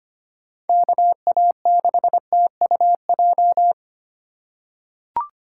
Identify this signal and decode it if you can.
Morse code, heard as a keyed tone, about 25 wpm, 700 Hz. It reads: KA6TUJ
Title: USA Call Signs - 3x Character Spacing - 25wpm